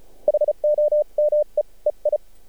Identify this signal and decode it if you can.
Morse code, heard as a keyed tone, about 34 wpm, spaced as for 30 wpm, 600 Hz. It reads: HOME EI
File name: archived / CRSnet